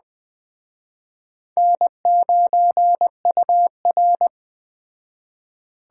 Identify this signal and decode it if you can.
Morse code, heard as a keyed tone, about 21 wpm, 700 Hz. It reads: N9UR